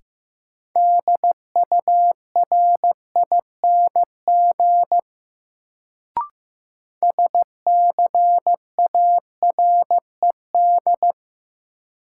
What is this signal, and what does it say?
Morse code, heard as a keyed tone, about 15 wpm, 700 Hz. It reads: DURING SCARED
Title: Sentences from Top 700 Words - New Word Drill - 15wpm